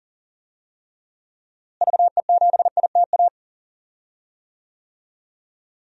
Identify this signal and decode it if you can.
Morse code, heard as a keyed tone, about 39 wpm, 700 Hz. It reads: VE7ITA
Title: DXCC-100 Call Signs - 6 Characters - Repeated 2x - 40wpm